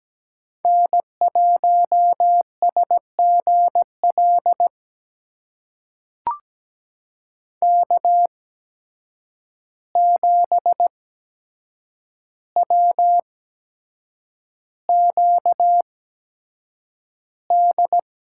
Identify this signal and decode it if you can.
Morse code, heard as a keyed tone, about 17 wpm, spaced as for 7 wpm, 700 Hz. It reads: N1SGL K7WQD
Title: USA Call Signs - 5 Characters - 8x Character Spacing - 17wpm